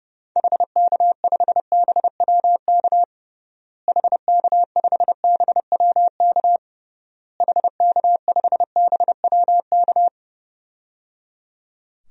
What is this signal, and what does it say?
Morse code, heard as a keyed tone, about 30 wpm, 700 Hz. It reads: HK5BWK HK5BWK HK5BWK
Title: DXCC-100 Call Signs - 6 Characters - Repeated 3x - 30wpm